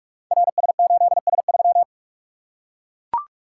Morse code, heard as a keyed tone, about 44 wpm, 700 Hz. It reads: RS8S3
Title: Code Groups - 5 Letter:Numbers - 45wpm